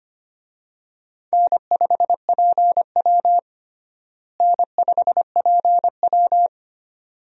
Morse code, heard as a keyed tone, about 25 wpm, 700 Hz. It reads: N5PW N5PW
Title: USA Call Signs - 4 Characters - Repeated 2x - 25wpm